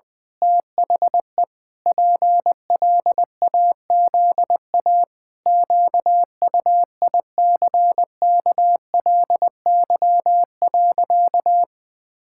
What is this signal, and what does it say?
Morse code, heard as a keyed tone, about 20 wpm, 700 Hz. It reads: THE PLAZA QUICKLY.
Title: Pangram Challenge v2 - 20wpm